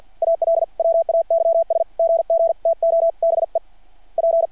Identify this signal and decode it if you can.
Morse code, heard as a keyed tone, about 50 wpm, 660 Hz. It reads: ALWAYS GG TO BE P
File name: archived / CRSnet